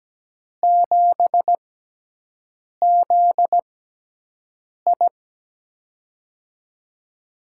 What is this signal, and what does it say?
Morse code, heard as a keyed tone, about 17 wpm, spaced as for 6 wpm, 700 Hz. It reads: 7ZI
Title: USA Call Signs - 4 Characters - 6x Character Spacing - 17wpm